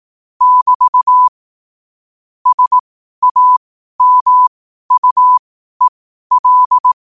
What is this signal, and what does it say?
Morse code, heard as a keyed tone, about 18 wpm, spaced as for 12 wpm, 1000 Hz. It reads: <BT> SAMUEL
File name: qotdfiles / 12wpm